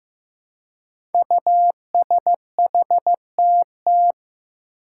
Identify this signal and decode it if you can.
Morse code, heard as a keyed tone, about 15 wpm, 700 Hz. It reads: USHTT
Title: Five Challenging Letter-Number Combinations 15wpm